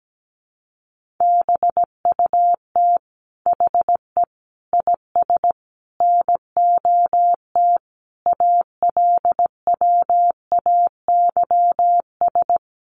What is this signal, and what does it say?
Morse code, heard as a keyed tone, about 17 wpm, 700 Hz. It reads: BUT HE IS NOT ALWAYS
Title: Sentences from Top 1000 Words 17wpm